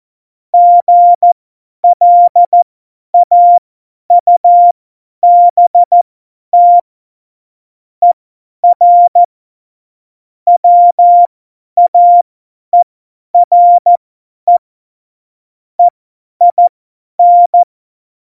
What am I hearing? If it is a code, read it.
Morse code, heard as a keyed tone, about 14 wpm, spaced as for 10 wpm, 700 Hz. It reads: GLAUBT ER WAERE EIN
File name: Morsekurs / Witze / Witz 22. August 17, Tempo 50 F70